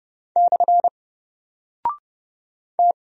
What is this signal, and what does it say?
Morse code, heard as a keyed tone, about 30 wpm, 700 Hz. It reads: / T
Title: Single Character (Letter:Number:Punctuation:Prosign) 30wpm